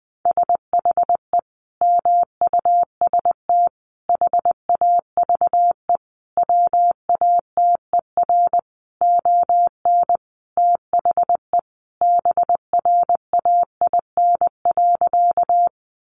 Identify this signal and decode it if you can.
Morse code, heard as a keyed tone, about 20 wpm, 700 Hz. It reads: SHE MUST HAVE WATER ON THE BRAIN.
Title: Sentences from Top 1100 Words 20wpm